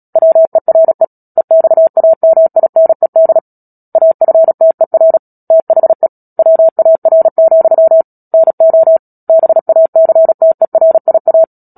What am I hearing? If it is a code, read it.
Morse code, heard as a keyed tone, about 37 wpm, 650 Hz. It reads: WERE EXAMINED AFTER THE WAR, NO BACTERIA
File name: Book Two / 10-Chapter 10 - Epilogue - 36 WPM